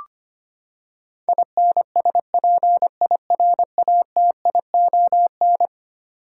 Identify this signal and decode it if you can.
Morse code, heard as a keyed tone, about 25 wpm, 700 Hz. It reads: INSPIRATION
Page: morse-code-ninja-practice-25wpm